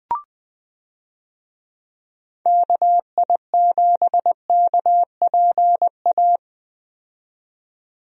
Morse code, heard as a keyed tone, about 21 wpm, 700 Hz. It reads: KI7KPA
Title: USA Call Signs - 6 Characters - 20wpm